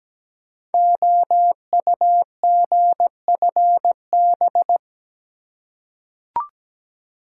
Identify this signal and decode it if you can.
Morse code, heard as a keyed tone, about 17 wpm, 700 Hz. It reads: OUGFB
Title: Code Groups - 5 Letters - 17wpm